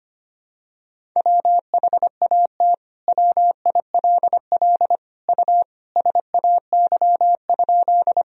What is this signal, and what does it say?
Morse code, heard as a keyed tone, about 25 wpm, 700 Hz. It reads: WHAT WILL U SAY?
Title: Sentences from Top 200 Words and Abbreviations 25wpm